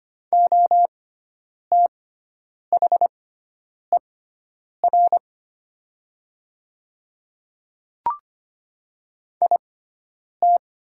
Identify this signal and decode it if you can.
Morse code, heard as a keyed tone, about 25 wpm, spaced as for 9 wpm, 700 Hz. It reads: OTHER IT
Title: 6x Character Spacing - Top 200 Words - 25wpm